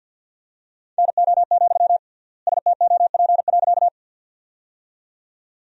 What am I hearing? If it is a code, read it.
Morse code, heard as a keyed tone, about 47 wpm, 700 Hz. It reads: NO, STOP.
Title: 3x Word-Spacing - Sentences from Top 300 Words - 50wpm